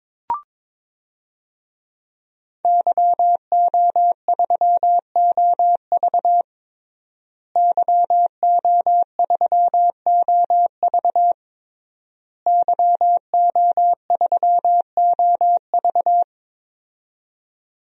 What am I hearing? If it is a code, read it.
Morse code, heard as a keyed tone, about 22 wpm, 700 Hz. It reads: YO3OV YO3OV YO3OV
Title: DXCC-100 Call Signs - 5 Characters - Repeated 3x - 22wpm